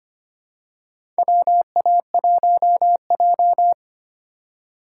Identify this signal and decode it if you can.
Morse code, heard as a keyed tone, about 25 wpm, 700 Hz. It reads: WA1J
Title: USA Call Signs - 4 Characters - 4x Character Spacing - 25wpm